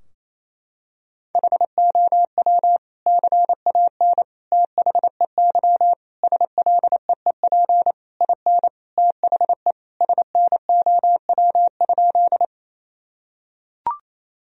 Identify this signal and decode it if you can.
Morse code, heard as a keyed tone, about 28 wpm, 700 Hz. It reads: HOW CAN THEY SLEEP IN THE SNOW?
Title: Sentences from Top 1000 Words - Review Part 3 - 28wpm